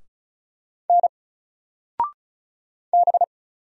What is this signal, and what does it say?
Morse code, heard as a keyed tone, about 36 wpm, 700 Hz. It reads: N B
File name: Single-Letter-Number-35wpm